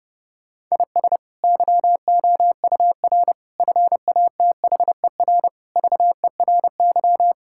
Morse code, heard as a keyed tone, about 30 wpm, 700 Hz. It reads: IS YOUR FATHER VERY
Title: Sentences from Top 1600 Words 30wpm